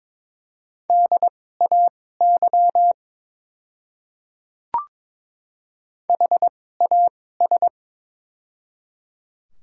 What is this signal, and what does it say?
Morse code, heard as a keyed tone, about 22 wpm, spaced as for 16 wpm, 700 Hz. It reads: DAY HAS
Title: 2x Character Spacing - Top 100 Words - 22wpm